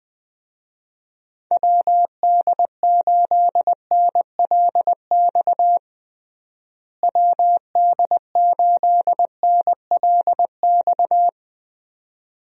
Morse code, heard as a keyed tone, about 20 wpm, 700 Hz. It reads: WD8NLX WD8NLX
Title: USA Call Signs - 6 Characters - Repeated 2x - 20wpm